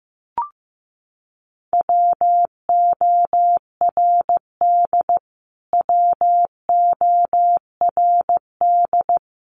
Morse code, heard as a keyed tone, about 15 wpm, 700 Hz. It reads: WORD WORD
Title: Top 300 Words Repeated 15wpm